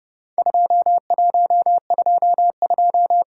Morse code, heard as a keyed tone, about 30 wpm, 700 Hz. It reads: 2122
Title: Worked All Europe DX Contest - QTC Exchange - 30wpm